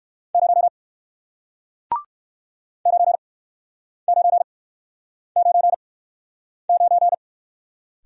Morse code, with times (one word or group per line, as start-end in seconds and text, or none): 0.3-0.7 0
2.9-7.1 9999